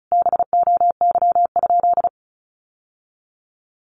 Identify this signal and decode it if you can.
Morse code, heard as a keyed tone, about 34 wpm, 700 Hz. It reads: BOY?
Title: Sentences from Top 1200 Words 35wpm